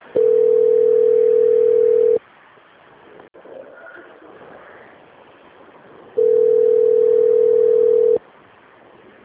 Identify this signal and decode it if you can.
Ringback tone (North America), 440 + 480 Hz.
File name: out- - - - - .5303